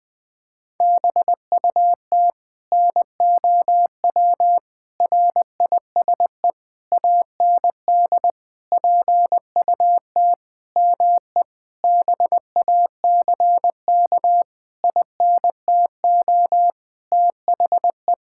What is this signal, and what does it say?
Morse code, heard as a keyed tone, about 20 wpm, 700 Hz. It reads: BUT NOW RISE AND PUT ME BACK INTO THE
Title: Sentences from Top 1000 Words - Review Part 6 - 20wpm